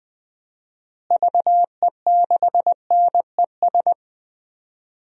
Morse code, heard as a keyed tone, about 21 wpm, 700 Hz. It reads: VE6NES
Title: DXCC-100 Call Signs - 6 Characters - Repeated 3x - 20wpm